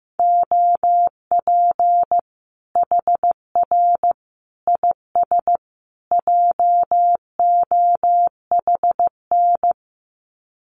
Morse code, heard as a keyed tone, about 15 wpm, 700 Hz. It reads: OP HR IS JOHN
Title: Common QSO Phrases 15wpm